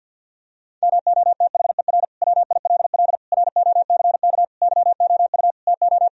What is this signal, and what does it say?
Morse code, heard as a keyed tone, about 50 wpm, 700 Hz. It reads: MOTHER WILL ROCK YOU TO
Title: Sentences from Top 1000 Words - Review Part 2 - 50wpm